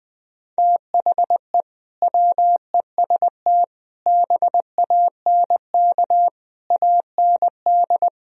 Morse code, heard as a keyed tone, about 20 wpm, 700 Hz. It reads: THE WEST BANK AND